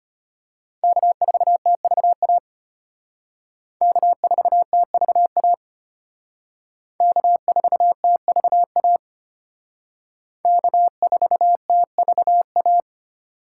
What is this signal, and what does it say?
Morse code, heard as a keyed tone, about 33 wpm, 700 Hz. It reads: K4TVA K4TVA K4TVA K4TVA